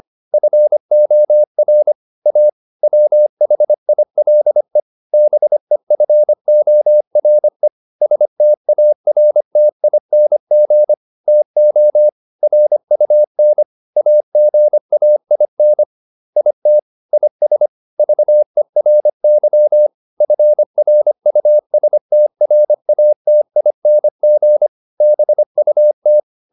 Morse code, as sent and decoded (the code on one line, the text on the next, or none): ..-. --- .-. / .- / .-- .... .. .-.. / -... ..-. --- .-. / ... - .- .-. - .. -. --. / - --- / .-. ..- -. / .- --. .- .. -. / .. - / .. ... / ...- . .-. -.-- / ..-. .-. ..- ... - .-. .- - .. -. --. / -... ..- -
FOR A WHILE BEFORE STARTING TO RUN AGAIN IT IS VERY FRUSTRATING BUT